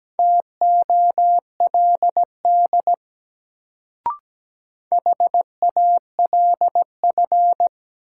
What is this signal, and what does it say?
Morse code, heard as a keyed tone, about 17 wpm, 700 Hz. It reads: TOLD HALF
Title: Sentences from Top 400 Words - New Word Drill - 17wpm